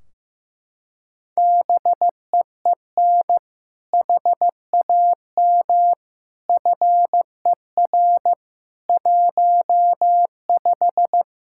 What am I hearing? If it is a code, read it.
Morse code, heard as a keyed tone, about 15 wpm, 700 Hz. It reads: BEEN HAM FER 15
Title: Practice QSO - Exchange 3 - More Info - 15wpm